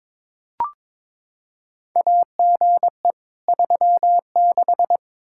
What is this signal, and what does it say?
Morse code, heard as a keyed tone, about 22 wpm, 700 Hz. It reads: AGE 36
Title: Practice QSO - Exchange 3 - More Info - 22wpm